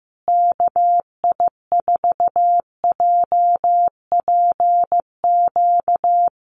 Morse code, heard as a keyed tone, about 15 wpm, 700 Hz. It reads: KI4JPQ
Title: USA Call Signs Repeated 3x 15wpm